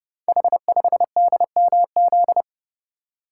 Morse code, heard as a keyed tone, about 29 wpm, 700 Hz. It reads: H5DMZ
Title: Code Groups - 5 Letter:Numbers - 30wpm